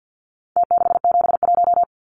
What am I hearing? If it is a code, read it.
Morse code, heard as a keyed tone, about 50 wpm, 700 Hz. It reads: T671
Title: Numbers - Contest Style - 4-Digit 50wpm